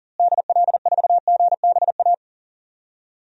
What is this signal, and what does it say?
Morse code, heard as a keyed tone, about 40 wpm, 700 Hz. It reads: DL4GBA
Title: DXCC-100 Call Signs - 6 Characters - Repeated 3x - 40wpm